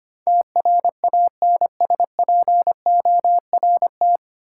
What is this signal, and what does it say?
Morse code, heard as a keyed tone, about 25 wpm, 700 Hz. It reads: TRANSPORT
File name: Nine Letter Words - 25wpm